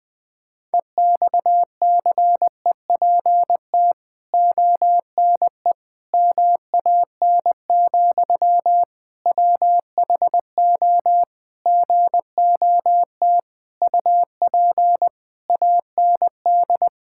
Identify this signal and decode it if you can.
Morse code, heard as a keyed tone, about 21 wpm, 700 Hz. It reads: EXCEPT ONE MAN, WHO GOT UP AND
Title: Sentences from Top 1600 Words 20wpm